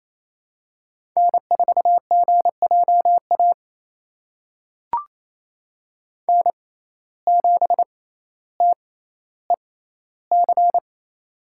Morse code, heard as a keyed tone, about 28 wpm, spaced as for 14 wpm, 700 Hz. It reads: N4GJA N7TEC
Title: USA Call Signs - 5 Characters - 6x Character Spacing - 28wpm